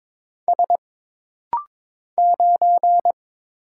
Morse code, heard as a keyed tone, about 22 wpm, 700 Hz. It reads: S 9
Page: This rapid-fire timing is geared to push your ICR (Instant Character Recognition) skills.